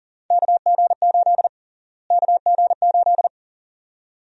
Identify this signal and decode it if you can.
Morse code, heard as a keyed tone, about 40 wpm, 700 Hz. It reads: KG8 KG8
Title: USA Call Sign Prefixes - Repeated 2x - 40wpm